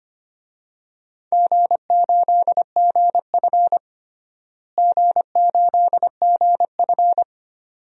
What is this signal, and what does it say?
Morse code, heard as a keyed tone, about 25 wpm, 700 Hz. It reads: G8GF G8GF
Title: DXCC-100 Call Signs - 4 Characters - Repeated 2x - 25wpm